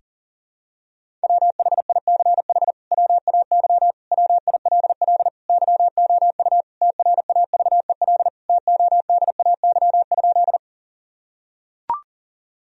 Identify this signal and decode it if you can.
Morse code, heard as a keyed tone, about 40 wpm, 700 Hz. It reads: WHICH WAY WILL YOU TRAVEL TODAY?
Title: Sentences from Top 1000 Words - Review Part 2 - 40wpm